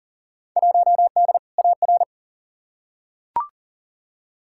Morse code, heard as a keyed tone, about 40 wpm, 700 Hz. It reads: 1D AR
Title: Field Day Exchanges - 40wpm